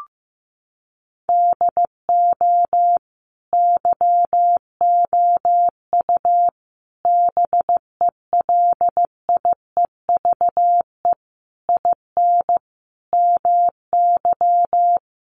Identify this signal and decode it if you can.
Morse code, heard as a keyed tone, about 15 wpm, 700 Hz. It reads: DO YOU BELIEVE IN MY
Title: Sentences from Top 1200 Words 15wpm